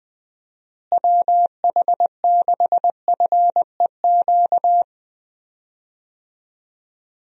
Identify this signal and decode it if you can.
Morse code, heard as a keyed tone, about 21 wpm, 700 Hz. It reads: WH6FEQ
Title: USA Call Signs - 6 Characters - Repeated 2x - 20wpm